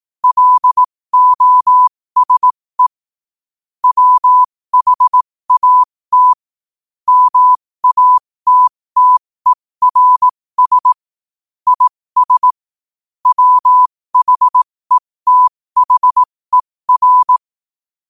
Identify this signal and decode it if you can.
Morse code, heard as a keyed tone, about 18 wpm, spaced as for 15 wpm, 1000 Hz. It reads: LOSE WHAT MATTERS IS WHETHER
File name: 2024-11-18-15wpm